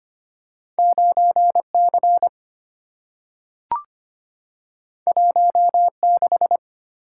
Morse code, heard as a keyed tone, about 25 wpm, 700 Hz. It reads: 9C 16
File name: Two Letter-Number Combinations - 25wpm